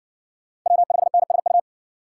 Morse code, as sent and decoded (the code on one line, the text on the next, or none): .-- ..... -. ... ..-
W5NSU